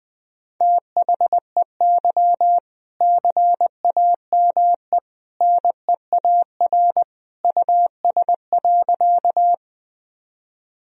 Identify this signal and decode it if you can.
Morse code, heard as a keyed tone, about 21 wpm, 700 Hz. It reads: THEY CAME NEAR US.
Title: Sentences from Top 1000 Words - Review Part 1 - 20wpm